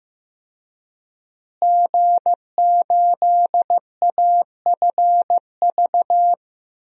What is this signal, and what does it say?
Morse code, heard as a keyed tone, about 15 wpm, 700 Hz. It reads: G8AFV
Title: DXCC-100 Call Signs - 5 Characters - Repeated 2x - 15wpm